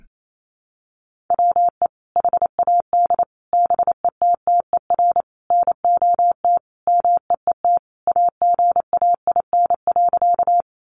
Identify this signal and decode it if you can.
Morse code, heard as a keyed tone, about 28 wpm, 700 Hz. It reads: WE HAD BETTER NOT MEET AGAIN.
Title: Sentences from Top 1100 Words 28wpm